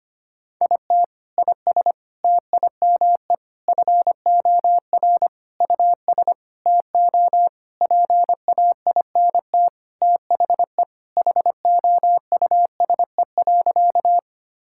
Morse code, heard as a keyed tone, about 25 wpm, 700 Hz. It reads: IT IS TIME FOR US TO PAINT THE HOUSE.
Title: Sentences from Top 1000 Words - Review Part 6 - 25wpm